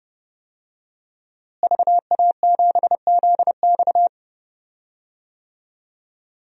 Morse code, heard as a keyed tone, about 30 wpm, 700 Hz. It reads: VA7ZX
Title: DXCC-100 Call Signs - 5 Characters - Repeated 2x - 30wpm